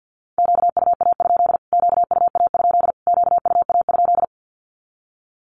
Morse code, heard as a keyed tone, about 48 wpm, 700 Hz. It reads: QUA? QUA? QUA?
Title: Q-Codes Repeated 3x 50wpm